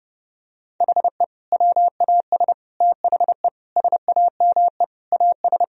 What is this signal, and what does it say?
Morse code, heard as a keyed tone, about 30 wpm, 700 Hz. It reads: HE WAS THE SAME AS
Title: Sentences from Top 1000 Words - Review Part 2 - 30wpm